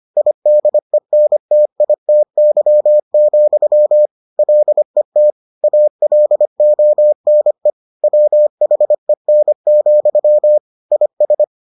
Morse code, heard as a keyed tone, about 25 wpm, 600 Hz. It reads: IDENTITY, LET ALONE WHEN, IS